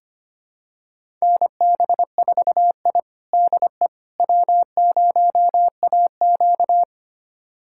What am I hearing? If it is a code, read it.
Morse code, heard as a keyed tone, about 25 wpm, 700 Hz. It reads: NB4I DE W0AQ
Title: USA Callsign DE Callsign - 4 Characters - 25wpm